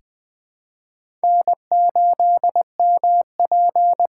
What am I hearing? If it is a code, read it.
Morse code, heard as a keyed tone, about 21 wpm, 700 Hz. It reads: N8MP
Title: USA Call Signs - 4 Characters - 3x Character Spacing - 20wpm